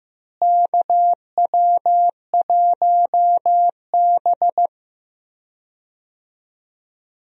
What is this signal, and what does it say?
Morse code, heard as a keyed tone, about 15 wpm, 700 Hz. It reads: KW1B
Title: USA Call Signs - 4 Characters - Repeated 3x - 15wpm